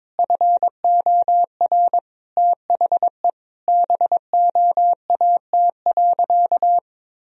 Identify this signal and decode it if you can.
Morse code, heard as a keyed tone, about 22 wpm, 700 Hz. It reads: FOR THE BOAT.
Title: Sentences from Top 1000 Words - Review Part 3 - 22wpm